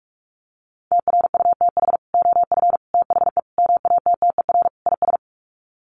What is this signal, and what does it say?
Morse code, heard as a keyed tone, about 44 wpm, 700 Hz. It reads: TRUTH OF THE MATTER IS
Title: Sets of 5 Words Encore2 45wpm